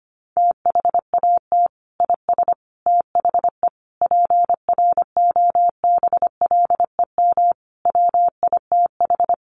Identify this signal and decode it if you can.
Morse code, heard as a keyed tone, about 25 wpm, 700 Hz. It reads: THAT IS THE PROBLEM WITH
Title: Sentences from Top 1100 Words 25wpm